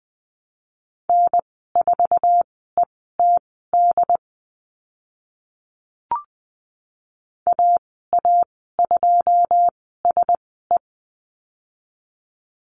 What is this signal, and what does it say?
Morse code, heard as a keyed tone, about 20 wpm, spaced as for 15 wpm, 700 Hz. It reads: N4ETD AA2SE
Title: 088 Lesson 15g - Number 5 - Call Signs 20wpm